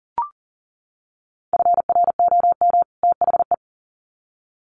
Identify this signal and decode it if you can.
Morse code, heard as a keyed tone, about 40 wpm, 700 Hz. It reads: FROM THE
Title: Sets of 2 Words 40wpm